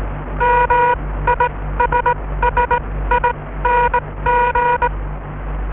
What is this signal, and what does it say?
Morse code, heard as a keyed tone, about 21 wpm, spaced as for 16 wpm, 960 Hz. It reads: MISSING